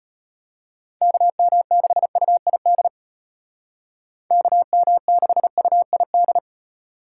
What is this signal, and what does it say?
Morse code, heard as a keyed tone, about 30 wpm, 700 Hz. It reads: KM6UID KM6UID
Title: Speed Racing - USA 2x3 Call Signs 25wpm